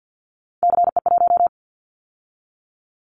Morse code, heard as a keyed tone, about 50 wpm, 700 Hz. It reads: CE1
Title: Callsign Prefixes for Top 100 DXCC - 50wpm